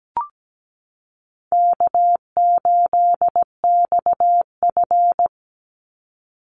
Morse code, heard as a keyed tone, about 17 wpm, 700 Hz. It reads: K8XF
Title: Call Signs - 17wpm